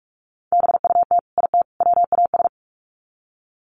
Morse code, heard as a keyed tone, about 44 wpm, 700 Hz. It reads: BUT IT WAS
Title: Sets of 3 Words 45wpm